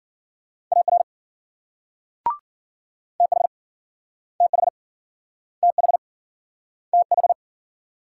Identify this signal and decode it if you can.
Morse code, heard as a keyed tone, about 53 wpm, 700 Hz. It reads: AR TH TH TH TH